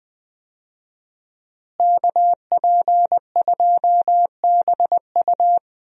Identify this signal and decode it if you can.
Morse code, heard as a keyed tone, about 21 wpm, 700 Hz. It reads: KP2BU
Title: USA Call Signs - 5 Characters - Repeated 3x - 20wpm